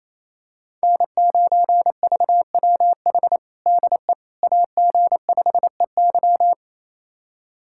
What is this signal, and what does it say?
Morse code, heard as a keyed tone, about 28 wpm, 700 Hz. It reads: N9VWH DE AG5EY
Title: USA Callsign DE Callsign - 5 Characters - 28wpm